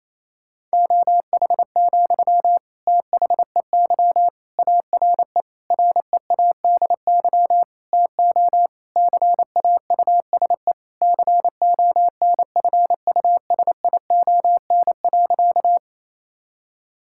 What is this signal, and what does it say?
Morse code, heard as a keyed tone, about 28 wpm, 700 Hz. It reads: OH, THEY ARE READY TO CAUSE CONFUSION.
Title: Sentences from Top 1600 Words 28wpm